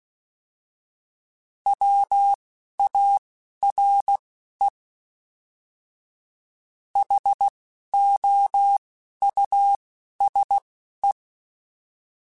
Morse code, heard as a keyed tone, about 16 wpm, spaced as for 11 wpm, 800 Hz. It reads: WARE HOUSE
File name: General / General-mp3s / Houses @10